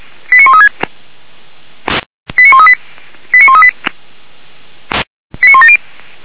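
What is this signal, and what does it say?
Five-tone selective call (ZVEI), heard as quick runs of five tones, tones of about 80 ms. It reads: 89136 89138 89138 89170
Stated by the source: ZVEI five-tone selective call 89135 received by SDR.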